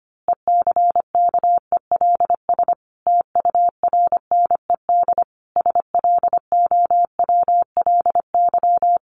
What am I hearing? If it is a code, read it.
Morse code, heard as a keyed tone, about 25 wpm, 700 Hz. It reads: ECKELS TURNED SLOWLY
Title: A Sound Of Thunder 25wpm